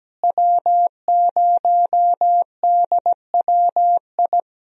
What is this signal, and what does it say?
Morse code, heard as a keyed tone, about 17 wpm, 700 Hz. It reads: W0DWI
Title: USA Call Signs - 5 Characters - Repeated 2x - 17wpm